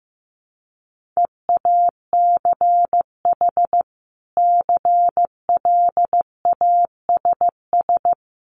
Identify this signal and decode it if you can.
Morse code, heard as a keyed tone, about 15 wpm, 700 Hz. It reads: EACH CLASS
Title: Sentences from Top 1400 Words 15wpm